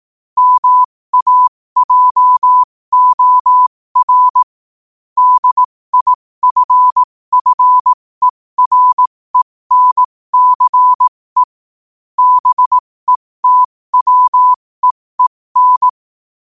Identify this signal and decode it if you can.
Morse code, heard as a keyed tone, about 18 wpm, spaced as for 15 wpm, 1000 Hz. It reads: MAJOR DIFFERENCE BETWEEN
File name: qotdfiles / 15wpm